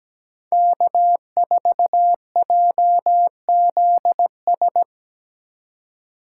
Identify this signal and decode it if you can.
Morse code, heard as a keyed tone, about 17 wpm, 700 Hz. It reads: K4JZS
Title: USA Call Signs - 5 Characters - 2x Character Spacing - 17wpm